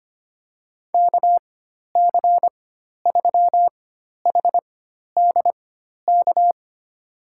Morse code, heard as a keyed tone, about 25 wpm, spaced as for 12 wpm, 700 Hz. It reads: KC3HDK
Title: USA Call Signs - 6 Characters - 4x Character Spacing - 25wpm